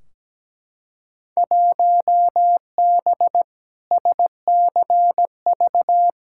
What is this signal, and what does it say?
Morse code, heard as a keyed tone, about 17 wpm, 700 Hz. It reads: 1B SCV
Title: Field Day Exchanges - 17wpm